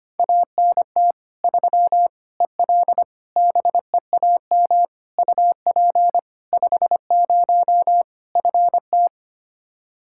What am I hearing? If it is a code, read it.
Morse code, heard as a keyed tone, about 25 wpm, 700 Hz. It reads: ANT 3 EL BEAM UP 50 FT
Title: Practice QSO - Exchange 2 - Info - 25wpm